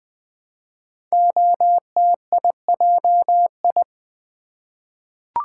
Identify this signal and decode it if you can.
Morse code, heard as a keyed tone, about 20 wpm, 700 Hz. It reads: OTIJI
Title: Code Groups - 5 Letters - 20wpm